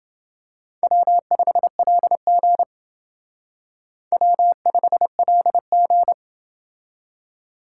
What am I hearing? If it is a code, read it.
Morse code, heard as a keyed tone, about 29 wpm, 700 Hz. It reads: W5LG W5LG